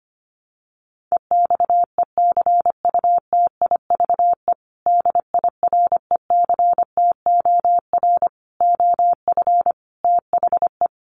Morse code, heard as a keyed tone, about 25 wpm, 700 Hz. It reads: EXECUTIVE DIRECTOR OF THE
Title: Sets of 5 Words Encore2 25wpm